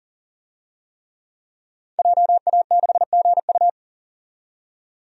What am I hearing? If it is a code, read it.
Morse code, heard as a keyed tone, about 40 wpm, 700 Hz. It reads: JA6GU
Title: DXCC-100 Call Signs - 5 Characters - 40wpm